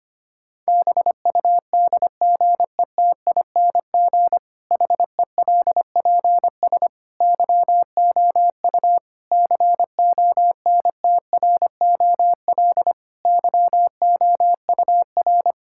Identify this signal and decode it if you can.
Morse code, heard as a keyed tone, about 25 wpm, 700 Hz. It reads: BUDGETING HELPS YOU CONTROL YOUR
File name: practice / 25wpm / Difficult Sentences with Hints - 25wpm